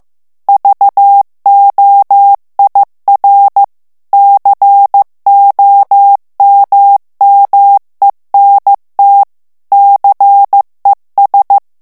Morse code, heard as a keyed tone, about 15 wpm, 800 Hz. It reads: VOIR COMMENT CES
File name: morse mp3 / FRANCAIS TEXTE CLAIR 2 VITESSE 14 MOTS MINUTES